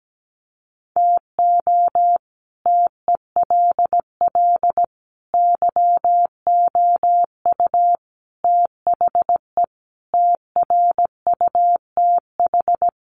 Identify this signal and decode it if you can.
Morse code, heard as a keyed tone, about 17 wpm, 700 Hz. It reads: TO TELL YOU THE TRUTH
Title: Sets of 5 Words 17wpm